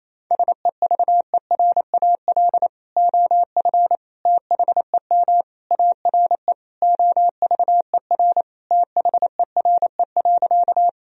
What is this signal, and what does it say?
Morse code, heard as a keyed tone, about 28 wpm, 700 Hz. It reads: SEVERAL OF THEM ARE OVER THERE.
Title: Sentences from Top 1000 Words - Review Part 2 - 28wpm